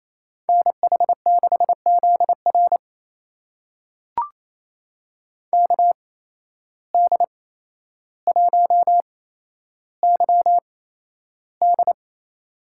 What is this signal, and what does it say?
Morse code, heard as a keyed tone, about 28 wpm, spaced as for 12 wpm, 700 Hz. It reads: NH6ZR KD1YD
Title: USA Call Signs - 5 Characters - 8x Character Spacing - 28wpm